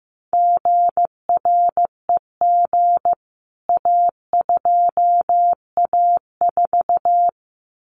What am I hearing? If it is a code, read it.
Morse code, heard as a keyed tone, about 15 wpm, 700 Hz. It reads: GREG A2A4
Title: CWT Exchange 15wpm